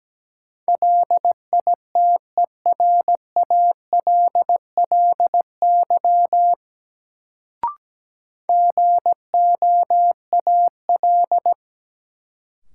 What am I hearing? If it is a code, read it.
Morse code, heard as a keyed tone, about 17 wpm, 700 Hz. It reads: LITERALLY GOAL
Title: Sentences from Top 1400 Words - New Word Drill Alt - 17wpm